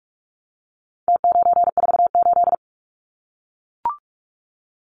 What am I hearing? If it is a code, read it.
Morse code, heard as a keyed tone, about 44 wpm, 700 Hz. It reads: T948
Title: Numbers - Contest Style - 4-Digit 45wpm